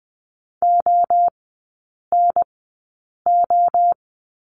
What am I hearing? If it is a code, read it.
Morse code, heard as a keyed tone, about 20 wpm, 700 Hz. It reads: O N O
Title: 029 Lesson 6e - Letter i - Three Characters 20wpm